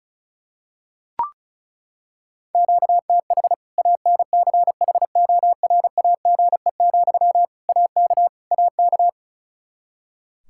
Morse code, heard as a keyed tone, about 36 wpm, 700 Hz. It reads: QTH ANCHORAGE, AK AK
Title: Practice QSO - Exchange 1 - Beginning - 35wpm